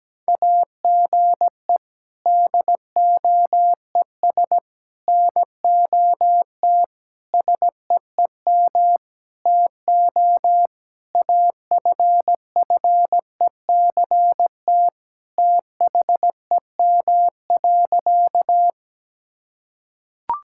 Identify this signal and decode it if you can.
Morse code, heard as a keyed tone, about 17 wpm, 700 Hz. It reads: AGE DOES NOT SEEM TO AFFECT THEM.
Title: Sentences from Top 1000 Words - Review Part 6 - 17wpm